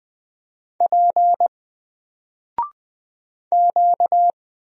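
Morse code, heard as a keyed tone, about 20 wpm, 700 Hz. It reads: P Q
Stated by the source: Single letter and number Morse drill at 20 wpm, from N4LCD.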